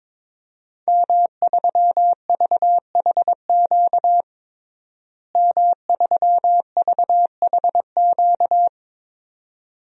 Morse code, heard as a keyed tone, about 22 wpm, 700 Hz. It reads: M3VHQ M3VHQ
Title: DXCC-100 Call Signs - 5 Characters - Repeated 2x - 22wpm